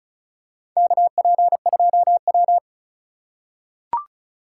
Morse code, heard as a keyed tone, about 34 wpm, 690 Hz. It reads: KP2W
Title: USA Call Signs - 4x Character Spacing - 35wpm